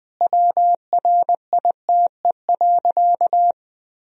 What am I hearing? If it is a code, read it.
Morse code, heard as a keyed tone, about 21 wpm, 700 Hz. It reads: WRITE.
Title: Sentences from Top 1000 Words - Review Part 1 - 20wpm